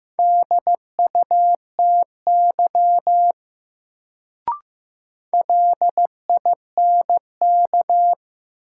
Morse code, heard as a keyed tone, about 15 wpm, 700 Hz. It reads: DUTY LINK
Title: Sentences from Top 1500 Words - New Word Drill Alt - 15wpm